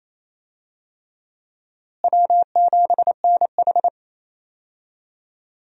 Morse code, heard as a keyed tone, about 28 wpm, 700 Hz. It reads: W7NH
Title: USA Call Signs - 4 Characters - Repeated 3x - 28wpm